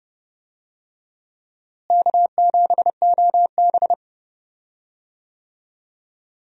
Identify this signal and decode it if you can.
Morse code, heard as a keyed tone, about 30 wpm, 700 Hz. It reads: K7OB
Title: USA Call Signs - 4 Characters - Repeated 2x - 30wpm